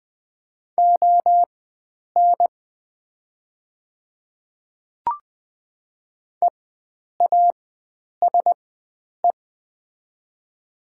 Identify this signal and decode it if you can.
Morse code, heard as a keyed tone, about 21 wpm, spaced as for 9 wpm, 700 Hz. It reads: ON EASE
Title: 4x Character Spacing - Top 300 Words - 20wpm